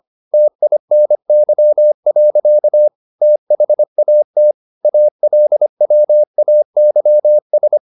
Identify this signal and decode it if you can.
Morse code, heard as a keyed tone, about 25 wpm, 600 Hz. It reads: TINY. THAT ALWAYS